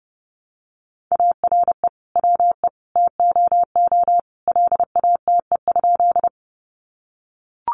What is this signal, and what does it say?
Morse code, heard as a keyed tone, about 30 wpm, 700 Hz. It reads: ARE WE TOO LATE?
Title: Sentences from Top 300 Words 30wpm